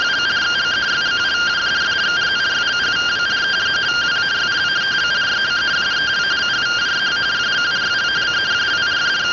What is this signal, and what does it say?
Two-tone teleprinter signal, about 170 Hz shift.
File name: findings / CT 205 ax mips, iDose 3 0003.gen-12 rtty rx